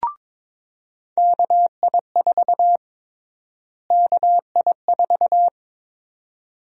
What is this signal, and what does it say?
Morse code, heard as a keyed tone, about 21 wpm, 700 Hz. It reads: KI4 KI4
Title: USA Call Sign Prefixes - Repeated 2x - 22wpm